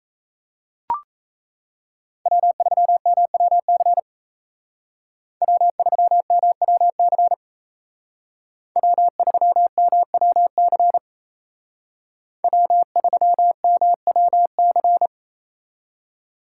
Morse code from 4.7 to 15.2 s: W3MWC W3MWC W3MWC

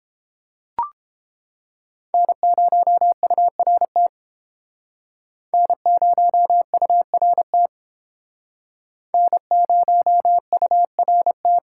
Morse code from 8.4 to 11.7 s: N0URT

